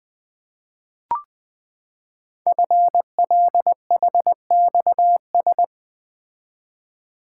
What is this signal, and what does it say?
Morse code, heard as a keyed tone, about 20 wpm, 700 Hz. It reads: FLHXS
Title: Code Groups - 5 Letters - 20wpm